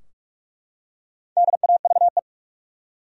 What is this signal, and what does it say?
Morse code, heard as a keyed tone, about 48 wpm, 700 Hz. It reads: DAVE
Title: USA Common Names - 45wpm